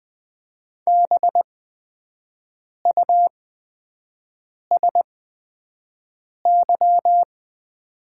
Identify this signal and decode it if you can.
Morse code, heard as a keyed tone, about 20 wpm, spaced as for 5 wpm, 700 Hz. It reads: BUSY
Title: 8x Character Spacing - Top 500 Words - 20wpm